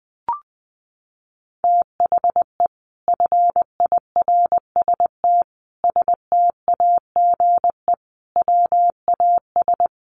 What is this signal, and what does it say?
Morse code, heard as a keyed tone, about 20 wpm, 700 Hz. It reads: THE FIRST STAGE WAS
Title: Sentences from Top 1200 Words 20wpm